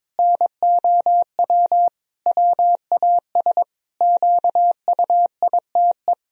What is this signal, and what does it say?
Morse code, heard as a keyed tone, about 22 wpm, 700 Hz. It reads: NOW WAS QUITE
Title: Sentences from Top 1000 Words - Review Part 3 - 22wpm